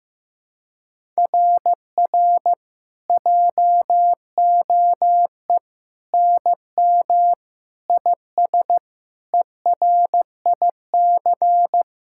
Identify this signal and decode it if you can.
Morse code, heard as a keyed tone, about 15 wpm, 700 Hz. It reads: RR JOE NM IS ERIC
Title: SOTA QSO Practice 15wpm